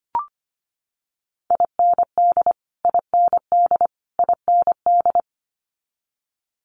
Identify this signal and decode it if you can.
Morse code, heard as a keyed tone, about 25 wpm, 700 Hz. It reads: IND IND IND
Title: Multi-letter Phonemes Repeated 3x 25wpm